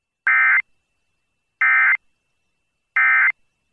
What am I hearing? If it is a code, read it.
Emergency Alert System (SAME) tones, heard as harsh buzzing bursts. It is heard from 0.2 to 3.3 s.